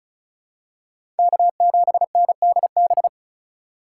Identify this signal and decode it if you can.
Morse code, heard as a keyed tone, about 36 wpm, 700 Hz. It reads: K7NDB
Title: USA Call Signs - 5 Characters - 2x Character Spacing - 35wpm